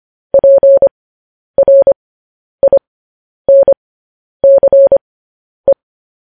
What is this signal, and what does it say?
Morse code, heard as a keyed tone, about 25 wpm, spaced as for 10 wpm, 550 Hz. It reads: PRINCE
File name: podcast / News headlines-1x-25x10